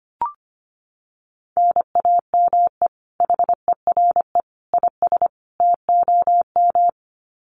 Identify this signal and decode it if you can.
Morse code, heard as a keyed tone, about 25 wpm, 700 Hz. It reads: NAME HERE IS TOM
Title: Common QSO Phrases 25wpm